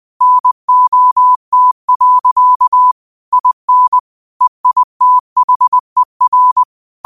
Morse code, heard as a keyed tone, about 20 wpm, 1000 Hz. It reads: NOT. IN EITHER
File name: qotdfiles / 20wpm